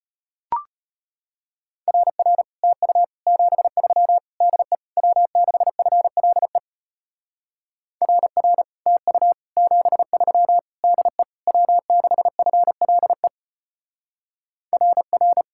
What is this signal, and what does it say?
Morse code, heard as a keyed tone, about 32 wpm, 700 Hz. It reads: RR TU 73 DE W6FLE RR TU 73 DE W6FLE RR